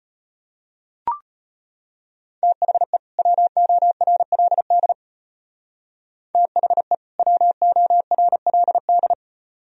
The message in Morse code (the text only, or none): THE WORLD THE WORLD